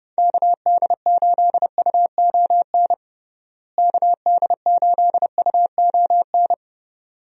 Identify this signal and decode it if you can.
Morse code, heard as a keyed tone, about 30 wpm, 700 Hz. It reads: KD8UON KD8UON
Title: USA Call Signs - 6 Characters - Repeated 2x - 30wpm